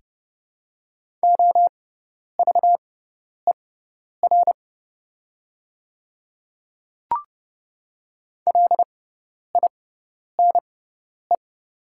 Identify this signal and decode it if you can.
Morse code, heard as a keyed tone, about 30 wpm, spaced as for 10 wpm, 700 Hz. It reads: OVER LINE
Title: 6x Character Spacing - Top 300 Words - 30wpm